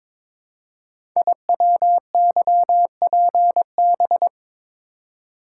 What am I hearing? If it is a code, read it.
Morse code, heard as a keyed tone, about 22 wpm, 700 Hz. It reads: IWYPB
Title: Code Groups - 5 Letters - 22wpm